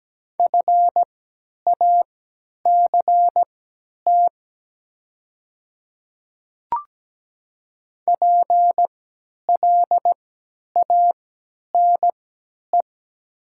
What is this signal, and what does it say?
Morse code, heard as a keyed tone, about 17 wpm, spaced as for 10 wpm, 700 Hz. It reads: FACT PLANE
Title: 3x Character Spacing - Top 500 Words - 17wpm